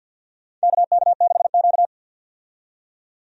Morse code, heard as a keyed tone, about 52 wpm, 700 Hz. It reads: KK6Q
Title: USA Call Signs - 4 Characters - 50wpm